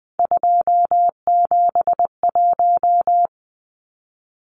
Morse code, heard as a keyed tone, about 20 wpm, 700 Hz. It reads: 271
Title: Numbers - 3-Digit 20wpm